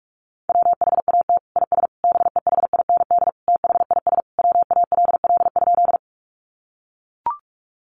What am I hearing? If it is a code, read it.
Morse code, heard as a keyed tone, about 44 wpm, 700 Hz. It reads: WHAT IS BEHIND THIS WALL?
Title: Sentences from Top 1100 Words 45wpm